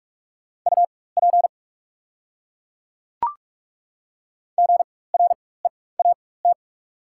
Morse code, heard as a keyed tone, about 44 wpm, spaced as for 21 wpm, 690 Hz. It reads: UP GREAT
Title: 4x Character Spacing - Top 200 Words - 45wpm